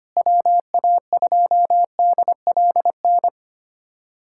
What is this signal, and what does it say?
Morse code, heard as a keyed tone, about 25 wpm, 700 Hz. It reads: WA2DLN
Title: USA Callsign DE Callsign - 6 Characters - 25wpm